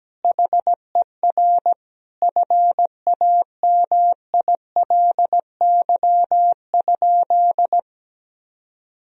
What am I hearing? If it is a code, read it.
Morse code, heard as a keyed tone, about 17 wpm, 700 Hz. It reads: HER FAMILY?
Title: Sentences from Top 1000 Words - Review Part 4 - 17wpm